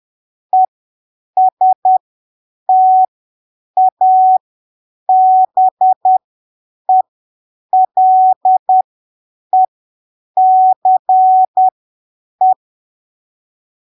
Morse code, heard as a keyed tone, about 10 wpm, spaced as for 7 wpm, 750 Hz. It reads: ESTABELECE
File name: arquivos / texto / texto 12 10ppm